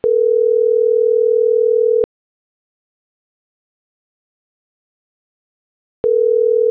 Ringback tone (North America), 440 + 480 Hz.